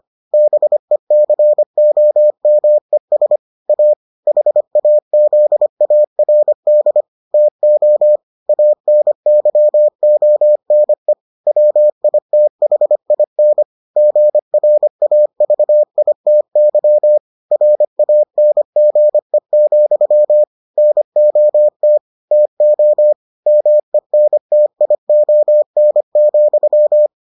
Morse code, heard as a keyed tone, about 25 wpm, 600 Hz. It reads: BECOMES A HAZARD TO ANYONE WITHIN GRAVITY RANGE, NOT TO MENTION,